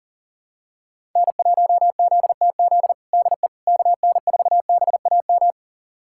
Morse code, heard as a keyed tone, about 40 wpm, 700 Hz. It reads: N1ZTZ DE KN4BAM
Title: USA Callsign DE Callsign 40wpm